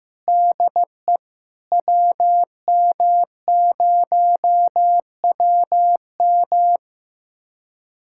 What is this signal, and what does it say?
Morse code, heard as a keyed tone, about 16 wpm, 700 Hz. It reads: DE WM0WM
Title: USA Callsign DE Callsign - 5 Characters - 15wpm